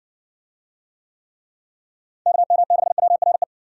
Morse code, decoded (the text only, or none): KM6WRE